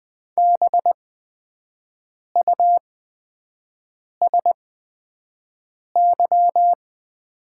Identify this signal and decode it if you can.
Morse code, heard as a keyed tone, about 20 wpm, spaced as for 5 wpm, 700 Hz. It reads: BUSY
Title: 8x Character Spacing - Top 500 Words - 20wpm